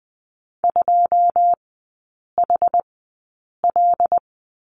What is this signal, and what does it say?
Morse code, heard as a keyed tone, about 20 wpm, spaced as for 8 wpm, 700 Hz. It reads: 2HL
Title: 087 Lesson 15e - Number 5 - Three Characters 20wpm